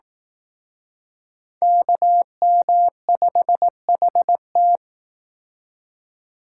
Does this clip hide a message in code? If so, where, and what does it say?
Yes: Morse code from 1.6 to 4.8 s, heard as a keyed tone, about 19 wpm, 700 Hz. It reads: KM5HT